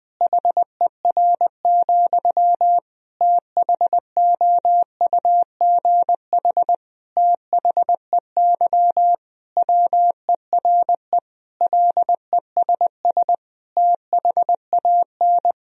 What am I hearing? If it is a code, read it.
Morse code, heard as a keyed tone, about 20 wpm, 700 Hz. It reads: HER, THOUGH THEY WERE LESS THAN
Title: Sentences from Top 1000 Words - Review Part 2 - 20wpm